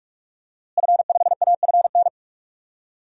Morse code, heard as a keyed tone, about 45 wpm, 700 Hz. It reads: F5AFN
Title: DXCC-100 Call Signs - 5 Characters - Repeated 3x - 45wpm